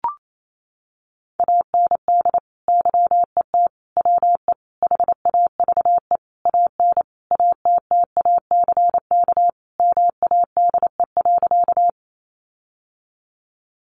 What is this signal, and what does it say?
Morse code, heard as a keyed tone, about 28 wpm, 700 Hz. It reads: AND YET WE HAVE AN ATTACK MADE.
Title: Sentences from Top 1500 Words 28wpm